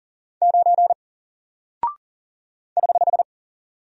Morse code, heard as a keyed tone, about 40 wpm, 700 Hz. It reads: 9 <HH>
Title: Single Character (Letter:Number:Punctuation:Prosign) Alt 40wpm